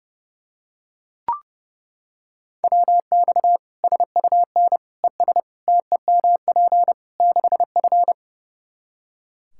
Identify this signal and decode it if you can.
Morse code, heard as a keyed tone, about 30 wpm, 700 Hz. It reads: WX SUN ES TEMP 6F
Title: Practice QSO - Exchange 2 - Info - 30wpm